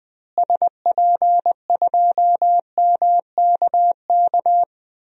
Morse code, heard as a keyed tone, about 20 wpm, 700 Hz. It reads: SP2MKK
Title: DXCC-100 Call Signs - 6 Characters - Repeated 3x - 20wpm